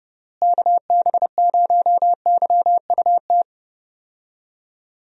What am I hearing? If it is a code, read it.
Morse code, heard as a keyed tone, about 29 wpm, 700 Hz. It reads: KB0YUT
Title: USA Call Signs - 6 Characters - Repeated 2x - 30wpm